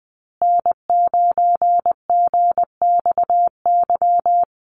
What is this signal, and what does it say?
Morse code, heard as a keyed tone, about 19 wpm, 700 Hz. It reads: N9GXY
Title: USA Call Signs Repeated 3x 20wpm